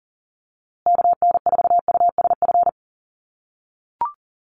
Morse code, heard as a keyed tone, about 40 wpm, 700 Hz. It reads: KN4USF
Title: USA Call Signs Repeated 2x 40wpm